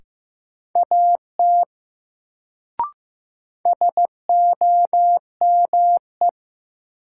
Morse code, heard as a keyed tone, about 15 wpm, 700 Hz. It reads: AT SOME
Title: Sets of 2 Words - New Word Drill - 15wpm